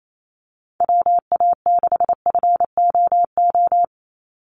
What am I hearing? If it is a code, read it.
Morse code, heard as a keyed tone, about 28 wpm, 700 Hz. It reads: WA6FOO
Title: USA Call Signs 28wpm